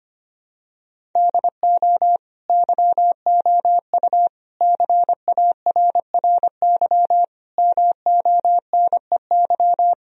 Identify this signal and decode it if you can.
Morse code, heard as a keyed tone, about 25 wpm, 700 Hz. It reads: DO YOU CARRY MONEY
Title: Sentences from Top 1000 Words - Review Part 1 - 25wpm